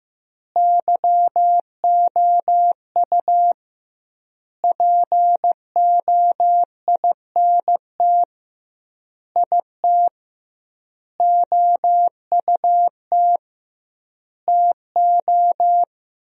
Morse code, heard as a keyed tone, about 15 wpm, 700 Hz. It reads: YOU POINT IT OUT TO
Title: 2x Word-Spacing - Sentences from Top 200 Words - 15wpm